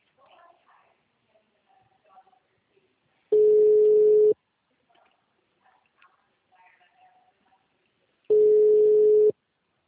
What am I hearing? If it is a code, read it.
Ringback tone (Europe, 425 Hz).
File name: out- -2023- - - .83399